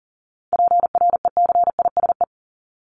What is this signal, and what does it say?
Morse code, heard as a keyed tone, about 40 wpm, 700 Hz. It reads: PRECISE
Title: LY Suffix - Top 100 Words with Base Words 40wpm